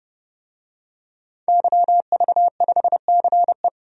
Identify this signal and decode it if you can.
Morse code, heard as a keyed tone, about 30 wpm, 700 Hz. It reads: YV5CE
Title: DXCC-100 Call Signs - 5 Characters - 30wpm